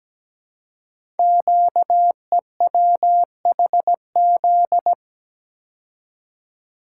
Morse code, heard as a keyed tone, about 17 wpm, 700 Hz. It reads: QEWHZ
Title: Code Groups - 5 Letter:Numbers - 17wpm